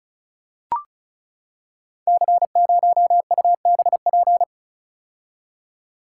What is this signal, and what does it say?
Morse code, heard as a keyed tone, about 36 wpm, 700 Hz. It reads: C0UBP
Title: Code Groups - 5 Letter:Numbers - 35wpm